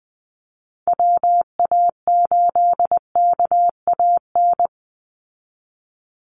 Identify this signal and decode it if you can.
Morse code, heard as a keyed tone, about 20 wpm, 700 Hz. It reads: WA8KAN
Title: Call Signs Repeated 3x - 20wpm